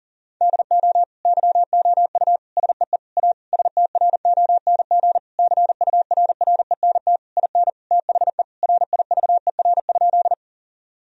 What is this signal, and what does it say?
Morse code, heard as a keyed tone, about 40 wpm, 700 Hz. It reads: DO YOU SEE A STRONG CURRENT IN THE RIVER?
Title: Sentences from Top 1000 Words - Review Part 4 - 40wpm